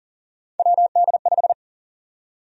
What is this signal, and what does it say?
Morse code, heard as a keyed tone, about 40 wpm, 700 Hz. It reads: WD5
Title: USA Call Sign Prefixes - Repeated 2x - 40wpm